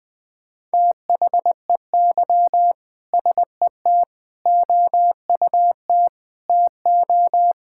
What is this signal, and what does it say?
Morse code, heard as a keyed tone, about 20 wpm, 700 Hz. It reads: THEY SET OUT TO
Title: Sentences with 2-Word Phrases - 20wpm